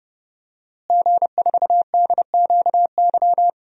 Morse code, heard as a keyed tone, about 30 wpm, 700 Hz. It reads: G4DQY
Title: DXCC-100 Call Signs - 5 Characters - Repeated 2x - 30wpm